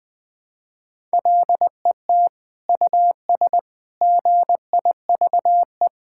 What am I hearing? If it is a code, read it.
Morse code, heard as a keyed tone, about 21 wpm, 700 Hz. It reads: LET US GIVE
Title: Sentences from Top 1000 Words - Review Part 5 - 20wpm